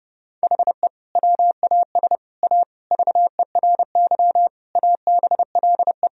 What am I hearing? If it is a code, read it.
Morse code, heard as a keyed tone, about 30 wpm, 700 Hz. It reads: HE WAS A VERY ABLE
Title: Sentences from Top 1000 Words - Review Part 2 - 30wpm